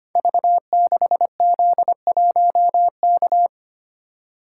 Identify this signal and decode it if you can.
Morse code, heard as a keyed tone, about 25 wpm, 700 Hz. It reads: V6Z1K
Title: Code Groups - 5 Letter:Numbers - 25wpm